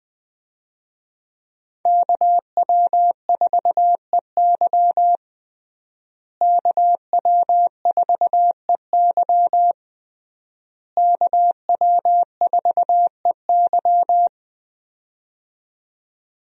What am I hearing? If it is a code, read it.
Morse code, heard as a keyed tone, about 20 wpm, 700 Hz. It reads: KW4EY KW4EY KW4EY
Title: USA Call Signs - 5 Characters - Repeated 3x - 20wpm